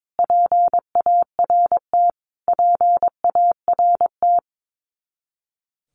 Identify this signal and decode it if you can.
Morse code, heard as a keyed tone, about 22 wpm, 700 Hz. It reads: PART PART
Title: Top 200 Words Repeated 22wpm